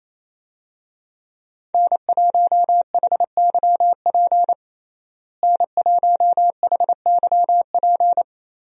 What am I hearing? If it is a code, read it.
Morse code, heard as a keyed tone, about 27 wpm, 700 Hz. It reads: N1HYP N1HYP
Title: USA Call Signs - 5 Characters - Repeated 2x - 28wpm